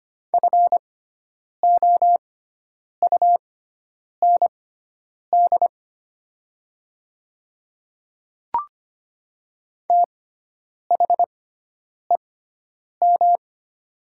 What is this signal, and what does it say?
Morse code, heard as a keyed tone, about 25 wpm, spaced as for 9 wpm, 700 Hz. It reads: FOUND THEM
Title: 6x Character Spacing - Top 300 Words - 25wpm